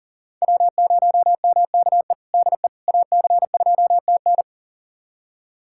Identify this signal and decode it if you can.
Morse code, heard as a keyed tone, about 40 wpm, 700 Hz. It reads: W0MKE DE AC2TN
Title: USA Callsign DE Callsign - 5 Characters - 40wpm